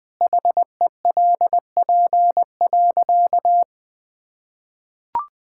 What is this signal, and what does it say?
Morse code, heard as a keyed tone, about 21 wpm, 700 Hz. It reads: HELP.
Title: Sentences with 2-Word Phrases - 20wpm